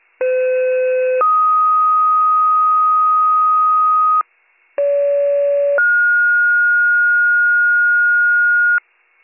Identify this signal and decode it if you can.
Two-tone page, heard as a one-second tone, then a three-second tone, about 510 then 1250 Hz, about 570 then 1430 Hz, with noise. It is heard from 0.2 to 8.8 s.